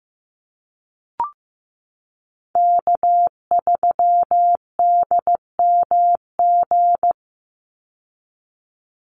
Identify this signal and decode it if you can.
Morse code, heard as a keyed tone, about 15 wpm, 700 Hz. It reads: K3DMG
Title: Call Signs Repeated 2x - 15wpm